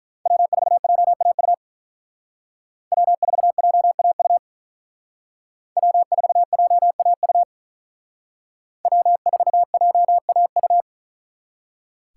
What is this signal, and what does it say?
Morse code, heard as a keyed tone, about 44 wpm, 700 Hz. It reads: W4JAU W4JAU W4JAU W4JAU